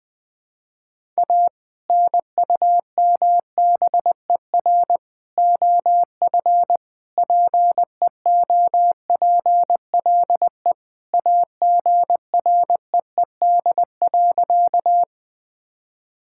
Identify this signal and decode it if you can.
Morse code, heard as a keyed tone, about 21 wpm, 700 Hz. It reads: A NUMBER OF PEOPLE AGREED.
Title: Sentences with 3-Word Phrases - Spoken Beginning - 20wpm